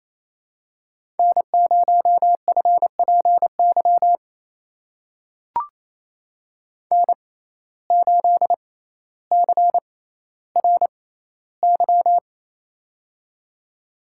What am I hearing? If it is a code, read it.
Morse code, heard as a keyed tone, about 28 wpm, spaced as for 14 wpm, 700 Hz. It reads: N0FPY N8CRY
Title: USA Call Signs - 5 Characters - 6x Character Spacing - 28wpm